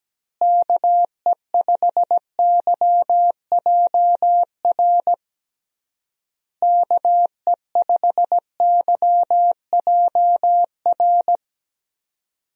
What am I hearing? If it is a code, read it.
Morse code, heard as a keyed tone, about 17 wpm, 700 Hz. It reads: KE5YJR KE5YJR
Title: USA Call Signs - 6 Characters - Repeated 3x - 17wpm